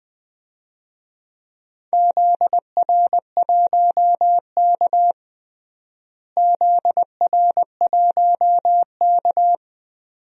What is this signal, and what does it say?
Morse code, heard as a keyed tone, about 20 wpm, 700 Hz. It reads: ZR1K ZR1K
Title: DXCC-100 Call Signs - 4 Characters - Repeated 2x - 20wpm